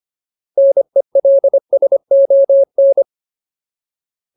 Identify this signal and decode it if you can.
Morse code, heard as a keyed tone, about 25 wpm, 550 Hz. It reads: NELSON
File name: Ultimate Word List / 25WPM Ultimate